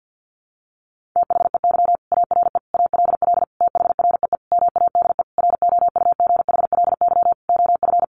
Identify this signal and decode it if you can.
Morse code, heard as a keyed tone, about 50 wpm, 700 Hz. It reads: THEY ARE ALL THREE MADE ROUGHLY OF
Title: Sentences from Top 1500 Words 50wpm